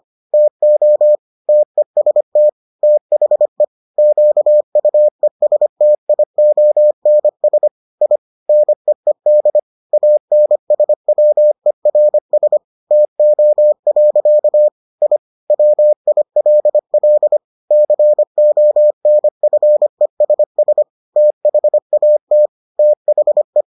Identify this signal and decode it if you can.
Morse code, heard as a keyed tone, about 25 wpm, 600 Hz. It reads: TO TEST THE QUESTIONS I NEED ANSWERS TO. I WILL CONFESS THAT THE